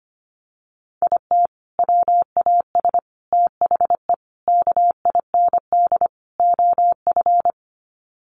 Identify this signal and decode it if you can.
Morse code, heard as a keyed tone, about 25 wpm, 700 Hz. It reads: IT WAS THE KIND OF
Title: Sets of 5 Words Encore2 25wpm